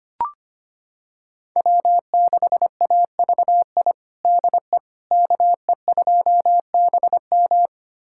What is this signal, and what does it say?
Morse code, heard as a keyed tone, about 25 wpm, 700 Hz. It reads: W6AVI DE KE2BM
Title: USA Callsign DE Callsign - 5 Characters - 25wpm